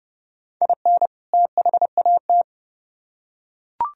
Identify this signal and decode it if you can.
Morse code, heard as a keyed tone, about 29 wpm, 700 Hz. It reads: IN THAT